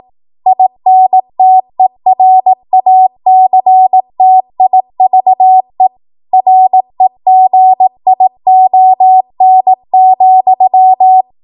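Morse code, heard as a keyed tone, about 18 wpm, 750 Hz. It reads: INTERACTIVE REGION,
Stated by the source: ARRL 18 wpm Morse code practice archive transmission.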